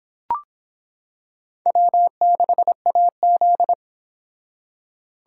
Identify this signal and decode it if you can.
Morse code, heard as a keyed tone, about 26 wpm, 700 Hz. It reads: W6AZ